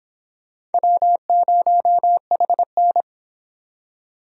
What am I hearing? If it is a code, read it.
Morse code, heard as a keyed tone, about 30 wpm, 700 Hz. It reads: W0HN